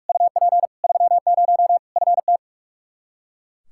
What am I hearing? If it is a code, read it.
Morse code, heard as a keyed tone, about 42 wpm, 700 Hz. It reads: UP 30 FT
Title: Practice QSO - Exchange 2 - Info - 45wpm